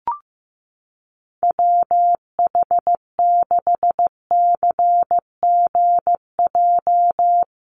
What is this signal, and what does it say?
Morse code, heard as a keyed tone, about 15 wpm, 700 Hz. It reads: WH6CGJ
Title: USA Call Signs Repeated 3x 15wpm